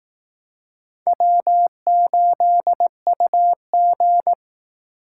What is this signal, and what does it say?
Morse code, heard as a keyed tone, about 18 wpm, 700 Hz. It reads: W8UG
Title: Speed Racing - USA Call Signs 15wpm